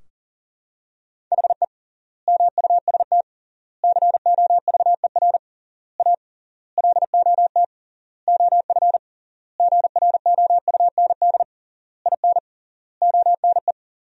Morse code, heard as a keyed tone, about 40 wpm, 700 Hz. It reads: HE MUST COVER A LOT OF GROUND IN ONE
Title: 3x Word-Spacing - Sentences from Top 500 Words - 40wpm